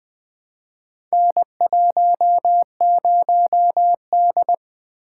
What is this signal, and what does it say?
Morse code, heard as a keyed tone, about 20 wpm, 700 Hz. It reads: N10D
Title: Four Letter-Number Combinations - 20wpm